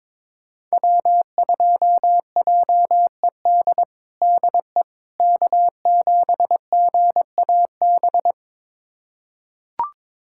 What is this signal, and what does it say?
Morse code, heard as a keyed tone, about 22 wpm, 700 Hz. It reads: W2JED DE K7GAB
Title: USA Callsign DE Callsign - 5 Characters - 22wpm